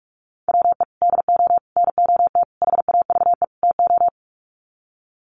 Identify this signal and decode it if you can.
Morse code, heard as a keyed tone, about 44 wpm, 700 Hz. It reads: WE DO NOT HAVE TO
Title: Sets of 5 Words 45wpm